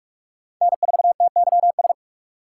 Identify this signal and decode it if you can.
Morse code, heard as a keyed tone, about 37 wpm, 700 Hz. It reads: N4TYS
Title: Speed Racing - USA 1x3 Call Signs 30wpm